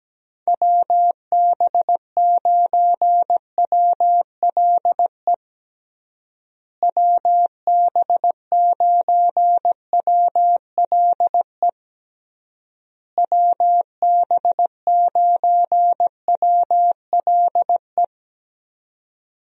Morse code, heard as a keyed tone, about 17 wpm, 700 Hz. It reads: WB9WLE WB9WLE WB9WLE
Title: USA Call Signs - 6 Characters - Repeated 3x - 17wpm